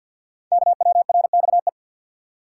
Morse code, heard as a keyed tone, about 48 wpm, 700 Hz. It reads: KWRXE
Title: Code Groups - 5 Letters - 50wpm